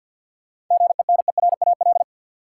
Morse code, heard as a keyed tone, about 50 wpm, 700 Hz. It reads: GENERAL
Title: Seven Letter Words - 50wpm